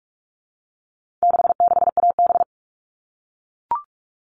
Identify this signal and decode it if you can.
Morse code, heard as a keyed tone, about 44 wpm, 700 Hz. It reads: 66AB
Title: All Cut Numbers - 4-Digit 45wpm